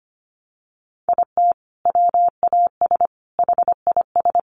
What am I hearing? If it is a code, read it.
Morse code, heard as a keyed tone, about 25 wpm, 700 Hz. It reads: IT WAS HIS
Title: Sentences from Top 1200 Words 25wpm